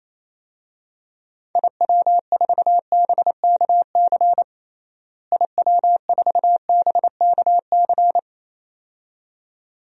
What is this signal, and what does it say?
Morse code, heard as a keyed tone, about 28 wpm, 700 Hz. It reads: IW4BKC IW4BKC
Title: DXCC-100 Call Signs - 6 Characters - Repeated 2x - 28wpm